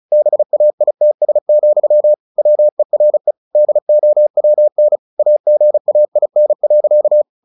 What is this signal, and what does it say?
Morse code, heard as a keyed tone, about 34 wpm, 600 Hz. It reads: BAITS, WERE DOWN AGAIN.
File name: the old man and the sea / the old man and the sea 1 35wpm